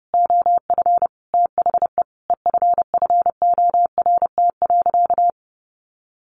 Morse code, heard as a keyed tone, about 30 wpm, 700 Hz. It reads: OF THE EFFORT.
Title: Sentences from Top 1200 Words 30wpm